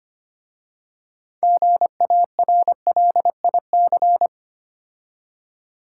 Morse code, heard as a keyed tone, about 25 wpm, 700 Hz. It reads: GARLIC
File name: practice / 25wpm / Difficult Words with Hints - 25wpm